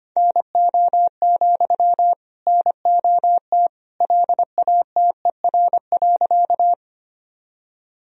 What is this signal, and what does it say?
Morse code, heard as a keyed tone, about 25 wpm, 700 Hz. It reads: NO, NOT LATER.
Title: Sentences from Top 1000 Words - Review Part 4 - 25wpm